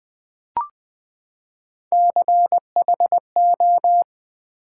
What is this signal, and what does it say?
Morse code, heard as a keyed tone, about 20 wpm, 700 Hz. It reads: CHO
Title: Build A Word - 3 Letters At A Time - 9-Letter Words - 20wpm